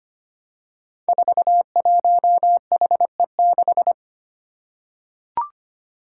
Morse code, heard as a keyed tone, about 25 wpm, 700 Hz. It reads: 41HE6
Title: Five Challenging Letter-Number Combinations 25wpm